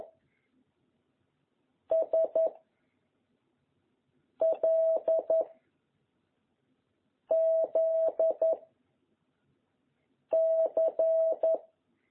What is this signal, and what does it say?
Morse code, heard as a keyed tone, about 11 wpm, 660 Hz. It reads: S L Z C
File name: jh8cbh / 2021cw